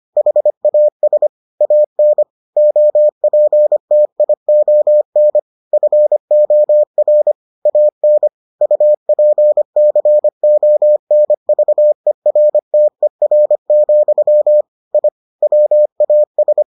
Morse code, heard as a keyed tone, about 25 wpm, 600 Hz. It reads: HAS AN OPTION FOR AN UPCONVERTER, I WAS